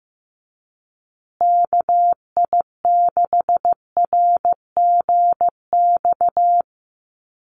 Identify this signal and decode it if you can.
Morse code, heard as a keyed tone, about 15 wpm, 700 Hz. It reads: KI6RGX
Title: USA Call Signs Repeated 2x 15wpm